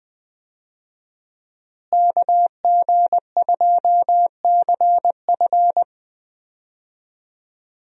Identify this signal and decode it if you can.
Morse code, heard as a keyed tone, about 21 wpm, 700 Hz. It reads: KG2CF
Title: USA Call Signs - 5 Characters - Repeated 3x - 20wpm